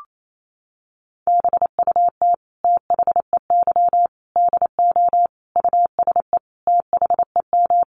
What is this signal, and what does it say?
Morse code, heard as a keyed tone, about 28 wpm, 700 Hz. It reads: BUT THEY DO USE THEM
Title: Sentences from Top 1300 Words 28wpm